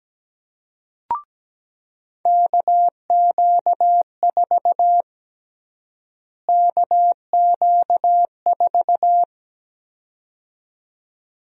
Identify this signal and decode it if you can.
Morse code, heard as a keyed tone, about 17 wpm, 700 Hz. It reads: KQ4 KQ4
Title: USA Call Sign Prefixes - Repeated 2x - 17wpm